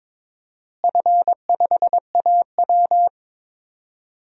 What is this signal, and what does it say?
Morse code, heard as a keyed tone, about 22 wpm, 700 Hz. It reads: F5AW
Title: DXCC-100 Call Signs - 4 Characters - Repeated 3x - 22wpm